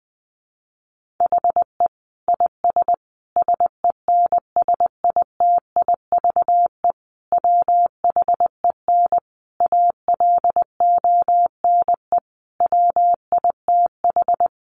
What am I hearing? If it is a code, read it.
Morse code, heard as a keyed tone, about 20 wpm, 700 Hz. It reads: HE IS SENSITIVE WHEN ALONE WITH
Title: Sentences from Top 1400 Words 20wpm